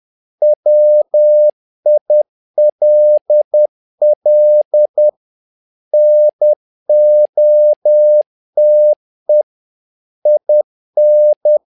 Morse code, heard as a keyed tone, about 10 wpm, 600 Hz. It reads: WILL NOTE IN
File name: learning center / tower trouble / tower trouble 10wpm